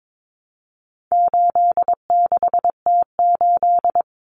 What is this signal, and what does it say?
Morse code, heard as a keyed tone, about 22 wpm, 700 Hz. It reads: 86T8
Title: Common Cut Numbers - 4-Digit 22wpm